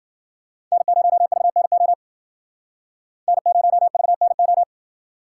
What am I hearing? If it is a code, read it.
Morse code, heard as a keyed tone, about 51 wpm, 700 Hz. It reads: N0VNO N0VNO